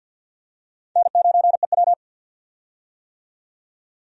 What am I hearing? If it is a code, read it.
Morse code, heard as a keyed tone, about 52 wpm, 700 Hz. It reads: N9EW
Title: USA Call Signs - 4 Characters - 50wpm